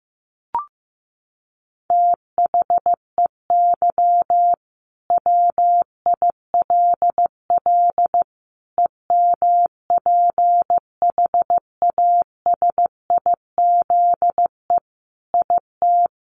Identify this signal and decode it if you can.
Morse code, heard as a keyed tone, about 15 wpm, 700 Hz. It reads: THEY WILL EMPHASIZE IT
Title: Sentences from Top 1500 Words 15wpm